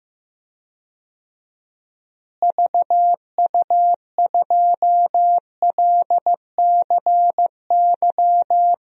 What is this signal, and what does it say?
Morse code, heard as a keyed tone, about 15 wpm, 700 Hz. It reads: VU2LCY
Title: DXCC-100 Call Signs - 6 Characters - Repeated 2x - 15wpm